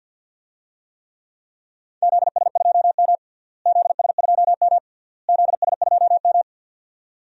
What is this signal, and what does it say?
Morse code, heard as a keyed tone, about 53 wpm, 700 Hz. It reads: ZS2M ZS2M ZS2M
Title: DXCC-100 Call Signs - 4 Characters - Repeated 3x - 50wpm